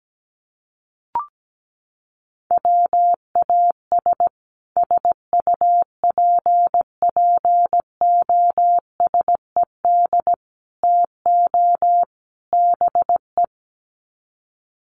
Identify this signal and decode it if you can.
Morse code, heard as a keyed tone, about 17 wpm, 700 Hz. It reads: WAS SUPPOSED TO BE
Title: Sets of 4 Words 17wpm